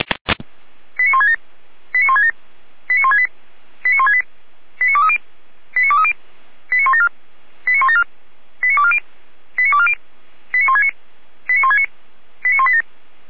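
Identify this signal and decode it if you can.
Five-tone selective call (ZVEI), heard as quick runs of five tones, tones of about 70 ms. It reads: 89167 89167 89168 89168 89230 89230 89174 89174 89240 89240 89169 89169 89187
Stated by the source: ZVEI five-tone call received on an SDR.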